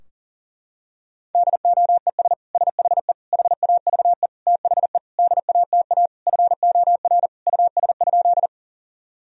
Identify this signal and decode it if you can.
Morse code, heard as a keyed tone, about 40 wpm, 700 Hz. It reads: DOES SHE HAVE THE DATA FOR US?
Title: Sentences from Top 1000 Words - Review Part 3 - 40wpm